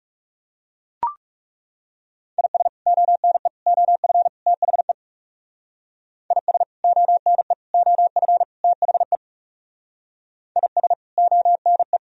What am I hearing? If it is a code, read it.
Morse code, heard as a keyed tone, about 35 wpm, 700 Hz. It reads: IS ONE OF THE IS ONE OF THE IS ONE
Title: Speed-Racing - Sets of 4 Words 30wpm